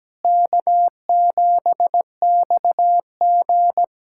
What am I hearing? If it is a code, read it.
Morse code, heard as a keyed tone, about 17 wpm, 700 Hz. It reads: K7XG
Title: USA Callsign DE Callsign - 4 Characters - 17wpm